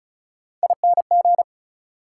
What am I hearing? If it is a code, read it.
Morse code, heard as a keyed tone, about 36 wpm, 700 Hz. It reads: ING
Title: Prefixes and Suffixes 35wpm